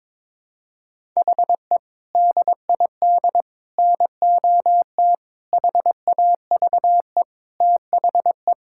Morse code, heard as a keyed tone, about 22 wpm, 700 Hz. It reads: HE DID NOT HAVE THE
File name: practice / 22wpm / Sentences with 4-Word Phrases - 22wpm